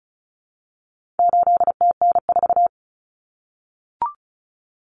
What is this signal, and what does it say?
Morse code, heard as a keyed tone, about 34 wpm, 700 Hz. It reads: 8TN4
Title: Common Cut Numbers - 4-Digit 35wpm